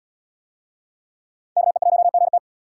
Morse code, heard as a keyed tone, about 77 wpm, 700 Hz. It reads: KE1GT